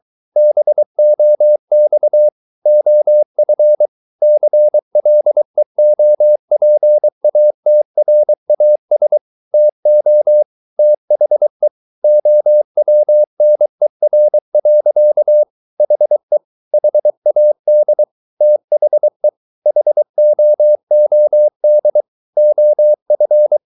Morse code, heard as a keyed tone, about 23 wpm, 600 Hz. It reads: BOX OF CLEOPATRAS TO THE OWNER. HE HAD THE HOOD OF
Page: Romanen Never af Ken Follet på engelsk.